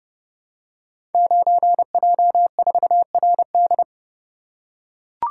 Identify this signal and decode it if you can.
Morse code, heard as a keyed tone, about 30 wpm, 700 Hz. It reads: 9J4RD
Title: Code Groups - 5 Letter:Numbers - 30wpm